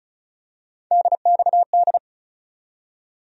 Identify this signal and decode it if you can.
Morse code, heard as a keyed tone, about 34 wpm, 700 Hz. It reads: DXD
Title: USA Call Sign Postfixes - 3 Letters 35wpm